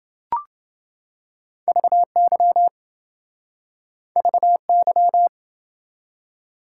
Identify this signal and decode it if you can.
Morse code, heard as a keyed tone, about 28 wpm, 700 Hz. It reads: VY VY